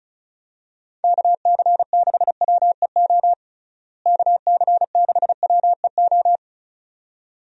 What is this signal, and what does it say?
Morse code, heard as a keyed tone, about 34 wpm, 700 Hz. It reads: KC6WEO KC6WEO
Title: USA Call Signs - 6 Characters - Repeated 2x - 35wpm